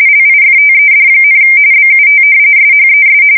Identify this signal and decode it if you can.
RTTY, about 45.45 baud. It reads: LIFTED, BUT SHES
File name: fsk170 / battleshipnj